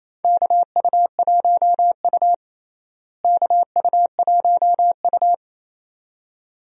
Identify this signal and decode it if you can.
Morse code, heard as a keyed tone, about 27 wpm, 700 Hz. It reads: KU1U KU1U
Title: USA Call Signs - 4 Characters - Repeated 2x - 28wpm